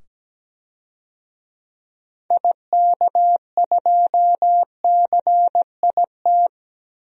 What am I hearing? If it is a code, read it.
Morse code, heard as a keyed tone, about 17 wpm, 700 Hz. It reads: IK2CIT
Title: DXCC-100 Call Signs - 6 Characters - 17wpm